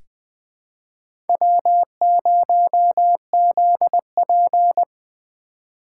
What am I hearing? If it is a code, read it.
Morse code, heard as a keyed tone, about 20 wpm, 700 Hz. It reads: W0ZP
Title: K1USN SST QSO Practice - 20wpm